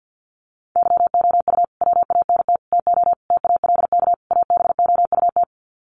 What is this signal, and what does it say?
Morse code, heard as a keyed tone, about 50 wpm, 700 Hz. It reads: YOU WANT TO TALK ABOUT
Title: Sets of 5 Words Encore 50wpm